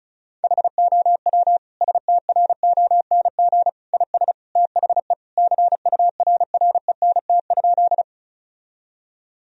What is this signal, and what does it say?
Morse code, heard as a keyed tone, about 36 wpm, 700 Hz. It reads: HOW STRONG IS THE CURRENT?
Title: Sentences from Top 1000 Words - Review Part 4 - 35wpm